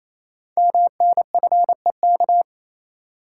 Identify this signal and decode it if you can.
Morse code, heard as a keyed tone, about 27 wpm, 700 Hz. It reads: MNFEK
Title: Code Groups - 5 Letters - 28wpm